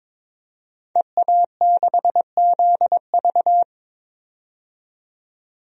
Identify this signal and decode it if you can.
Morse code, heard as a keyed tone, about 22 wpm, 700 Hz. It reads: EA6ZV
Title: DXCC-100 Call Signs - 5 Characters - 22wpm